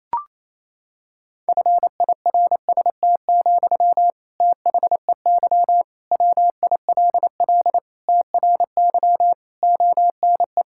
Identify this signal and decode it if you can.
Morse code, heard as a keyed tone, about 28 wpm, 700 Hz. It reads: FIRST, THEY WILL TRY ONE
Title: Sentences from Top 1000 Words - Review Part 6 - 28wpm